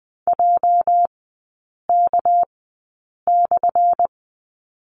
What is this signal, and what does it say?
Morse code, heard as a keyed tone, about 20 wpm, spaced as for 8 wpm, 690 Hz. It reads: JK/
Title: 212 Lesson 34e - Number 8 - Three Characters 20wpm